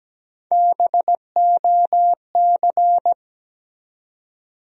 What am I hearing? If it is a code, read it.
Morse code, heard as a keyed tone, about 17 wpm, 700 Hz. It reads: BOC
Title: USA Call Sign Postfixes - 3 Letters 17wpm